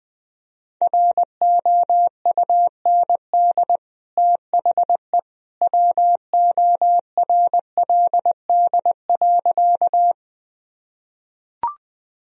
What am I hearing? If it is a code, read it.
Morse code, heard as a keyed tone, about 20 wpm, 700 Hz. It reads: ROUND THE WORLD.
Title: Sentences from Top 1000 Words - Review Part 1 - 20wpm